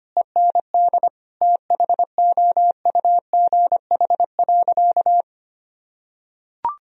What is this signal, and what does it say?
Morse code, heard as a keyed tone, about 25 wpm, 700 Hz. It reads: END THOUGH.
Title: Sentences from Top 1000 Words - Review Part 2 - 25wpm